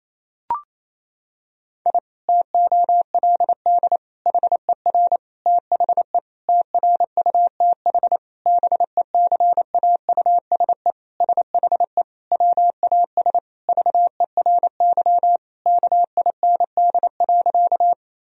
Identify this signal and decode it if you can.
Morse code, heard as a keyed tone, about 28 wpm, 700 Hz. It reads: I TOLD HER THE TRUTH BECAUSE SHE WAS VERY KIND.
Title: Sentences from Top 1000 Words - Review Part 3 - 28wpm